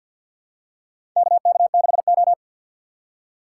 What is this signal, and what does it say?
Morse code, heard as a keyed tone, about 51 wpm, 700 Hz. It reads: KK6O
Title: USA Call Signs - 4 Characters - 2x Character Spacing - 50wpm